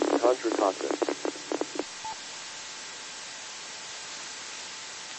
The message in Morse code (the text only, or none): none